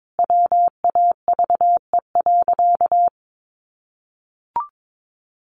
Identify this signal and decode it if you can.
Morse code, heard as a keyed tone, about 22 wpm, 700 Hz. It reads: WAVE.
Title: Sentences from Top 1400 Words 22wpm